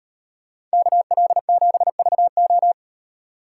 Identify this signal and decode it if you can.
Morse code, heard as a keyed tone, about 38 wpm, 700 Hz. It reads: KL7VO